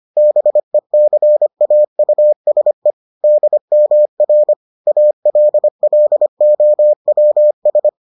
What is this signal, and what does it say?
Morse code, heard as a keyed tone, about 25 wpm, 600 Hz. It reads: BECAUSE DMR ALLOWS